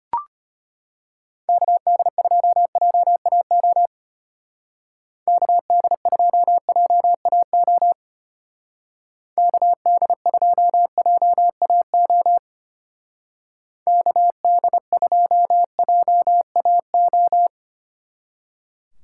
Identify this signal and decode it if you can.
Morse code, heard as a keyed tone, about 32 wpm, 700 Hz. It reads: KD2JAO KD2JAO KD2JAO KD2JAO